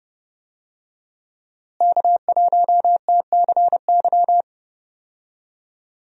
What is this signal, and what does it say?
Morse code, heard as a keyed tone, about 30 wpm, 700 Hz. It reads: K1TCY
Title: USA Call Signs - 5 Characters - Repeated 3x - 30wpm